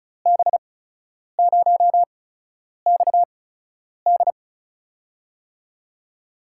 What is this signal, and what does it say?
Morse code, heard as a keyed tone, about 36 wpm, spaced as for 10 wpm, 700 Hz. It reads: B0XD
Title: USA Call Signs - 8x Character Spacing - 35wpm